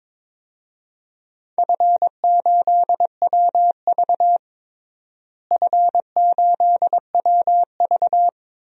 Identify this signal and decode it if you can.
Morse code, heard as a keyed tone, about 22 wpm, 700 Hz. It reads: F8WV F8WV
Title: DXCC-100 Call Signs - 4 Characters - Repeated 3x - 22wpm